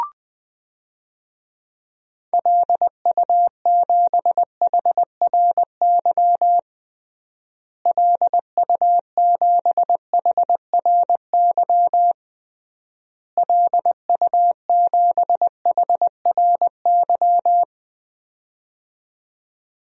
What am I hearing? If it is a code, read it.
Morse code, heard as a keyed tone, about 20 wpm, 700 Hz. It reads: LU7HRY LU7HRY LU7HRY
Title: DXCC-100 Call Signs - 6 Characters - Repeated 3x - 20wpm